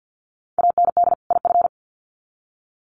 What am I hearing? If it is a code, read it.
Morse code, heard as a keyed tone, about 50 wpm, 700 Hz. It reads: AND IF
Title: Sets of 2 Words 50wpm